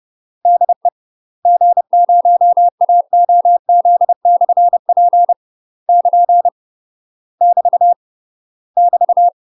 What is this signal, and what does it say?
Morse code, heard as a keyed tone, about 30 wpm, 700 Hz. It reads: DE G0AOZ/P <KN> <BT> <BT>
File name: UK MASTER QSOs TRAINING / 30WPM / UK MASTER QSOs 0002